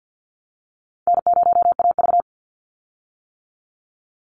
Morse code, heard as a keyed tone, about 50 wpm, 700 Hz. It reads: N0AV
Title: Easy Call Signs - 50wpm